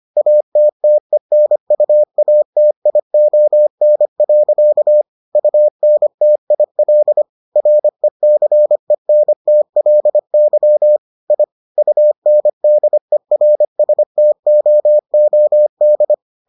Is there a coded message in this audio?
Yes: Morse code from 0.2 to 16.1 s, heard as a keyed tone, about 25 wpm, 600 Hz. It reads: ATTENUATION. UNTIL RECENTLY I UNDERSTOOD